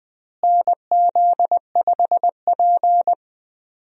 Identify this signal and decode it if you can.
Morse code, heard as a keyed tone, about 20 wpm, 700 Hz. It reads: NZ5P
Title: USA Call Signs - 4 Characters - Repeated 2x - 20wpm